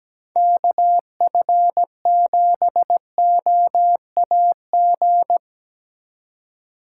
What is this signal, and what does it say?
Morse code, heard as a keyed tone, about 17 wpm, 700 Hz. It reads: KF7OAG
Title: USA Call Signs - 6 Characters - 3x Character Spacing - 17wpm